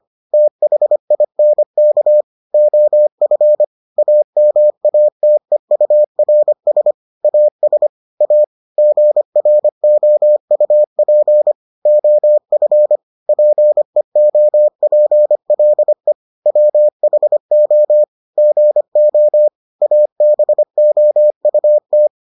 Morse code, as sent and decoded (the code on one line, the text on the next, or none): - .... .. -. -.- / --- ..-. / .- -- .- - . ..- .-. ... / .- ... / .- / --. .-. --- ..- .--. / --- ..-. / .--. --- .--. .-.. / .-- .... --- / --. --- / .- -... --- ..- -
THINK OF AMATEURS AS A GROUP OF PEOPLE WHO GO ABOUT